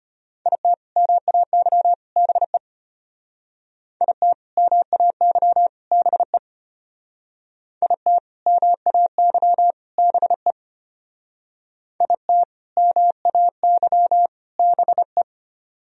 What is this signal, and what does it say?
Morse code, heard as a keyed tone, about 32 wpm, 700 Hz. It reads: IT MAY BE IT MAY BE IT MAY BE IT MAY BE